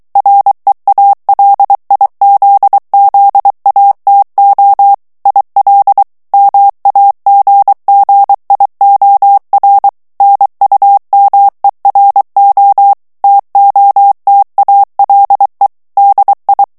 Morse code, heard as a keyed tone, about 23 wpm, 800 Hz. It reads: REALIZZATO IL MAGGIOR NUMERO TOTALE DI
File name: ITALIEN texte clair 2 vitesse 22 mots minute